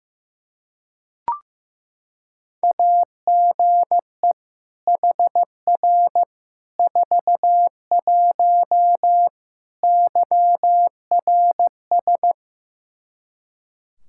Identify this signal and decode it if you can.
Morse code, heard as a keyed tone, about 15 wpm, 700 Hz. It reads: AGE HR 41 YRS
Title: Practice QSO - Exchange 3 - More Info - 15wpm